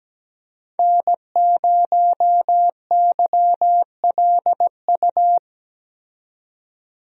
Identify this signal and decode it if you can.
Morse code, heard as a keyed tone, about 17 wpm, 700 Hz. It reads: N0YLU
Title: USA Call Signs - 5 Characters - 17wpm